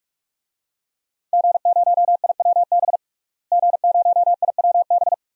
Morse code, heard as a keyed tone, about 46 wpm, 700 Hz. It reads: G0IWB G0IWB
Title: DXCC-100 Call Signs - 5 Characters - Repeated 2x - 45wpm